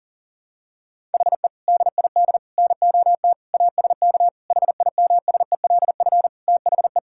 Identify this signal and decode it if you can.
Morse code, heard as a keyed tone, about 40 wpm, 700 Hz. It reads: HE DID NOT ASK HIMSELF THE
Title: Sentences from Top 1600 Words 40wpm